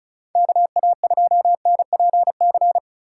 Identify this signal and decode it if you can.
Morse code, heard as a keyed tone, about 34 wpm, 700 Hz. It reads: KA2NPC
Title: USA Call Signs - 4x Character Spacing - 35wpm